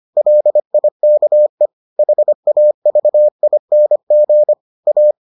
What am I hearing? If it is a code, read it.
Morse code, heard as a keyed tone, about 25 wpm, 600 Hz. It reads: LIKE HAVING A